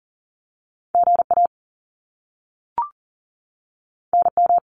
Morse code, heard as a keyed tone, about 41 wpm, 700 Hz. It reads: GA NM
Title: US States 40wpm